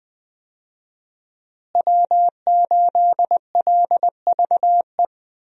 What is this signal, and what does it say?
Morse code, heard as a keyed tone, about 20 wpm, 700 Hz. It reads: W8LVE
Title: USA Call Signs - 5 Characters - Repeated 3x - 20wpm